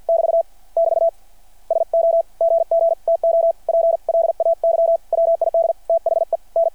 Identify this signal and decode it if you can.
Morse code, heard as a keyed tone, about 52 wpm, spaced as for 44 wpm, 660 Hz. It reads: <BT> <BT> SO GG TO PLAY WID THEN
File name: archived / CRSnet